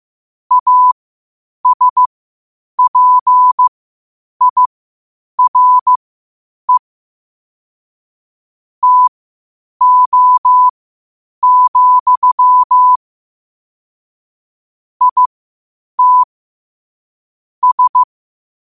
Morse code, heard as a keyed tone, about 15 wpm, spaced as for 8 wpm, 1000 Hz. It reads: ASPIRE TO, IT S
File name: qotdfiles / 8wpm